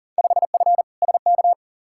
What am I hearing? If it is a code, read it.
Morse code, heard as a keyed tone, about 38 wpm, 700 Hz. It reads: 5F SK
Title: Field Day Exchanges Encore - 40wpm